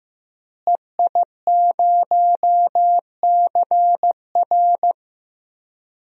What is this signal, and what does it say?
Morse code, heard as a keyed tone, about 15 wpm, 700 Hz. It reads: EI0CR
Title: DXCC-100 Call Signs - 5 Characters - Repeated 2x - 15wpm